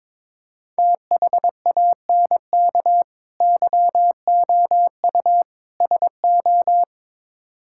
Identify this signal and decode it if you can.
Morse code, heard as a keyed tone, about 23 wpm, 700 Hz. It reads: THANK YOU SO
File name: practice / 22wpm / Build A Phrase - 5-Word Phrases - 22wpm